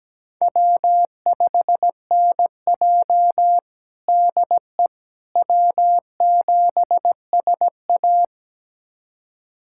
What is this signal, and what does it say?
Morse code, heard as a keyed tone, about 17 wpm, 700 Hz. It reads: W5NJ DE W7SA
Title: USA Callsign DE Callsign - 4 Characters - 17wpm